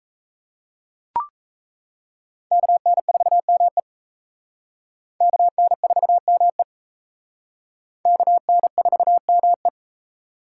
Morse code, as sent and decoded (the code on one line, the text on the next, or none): -.- -. ....- -- . / -.- -. ....- -- . / -.- -. ....- -- .
KN4ME KN4ME KN4ME